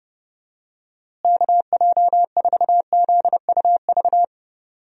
Morse code, heard as a keyed tone, about 30 wpm, 700 Hz. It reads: KJ4ZUV
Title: USA Call Signs - 6 Characters - Repeated 2x - 30wpm